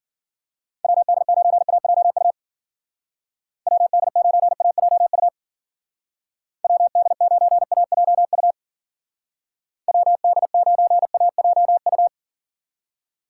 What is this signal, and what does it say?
Morse code, heard as a keyed tone, about 49 wpm, 700 Hz. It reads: WD9AJU WD9AJU WD9AJU WD9AJU
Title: Speed Racing - USA 2x3 Call Signs 40wpm